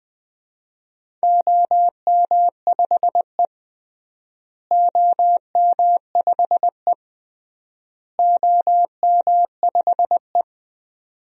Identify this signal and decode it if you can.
Morse code, heard as a keyed tone, about 20 wpm, 700 Hz. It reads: OM5E OM5E OM5E
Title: DXCC-100 Call Signs - 4 Characters - Repeated 3x - 20wpm